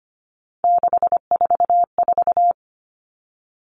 Morse code, heard as a keyed tone, about 25 wpm, 700 Hz. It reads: 644
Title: Numbers - 3-Digit 25wpm